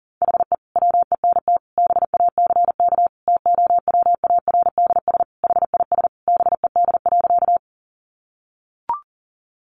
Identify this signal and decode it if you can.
Morse code, heard as a keyed tone, about 40 wpm, 700 Hz. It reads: HE WENT BACK TOWARDS HIS BED.
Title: Sentences from Top 1100 Words 40wpm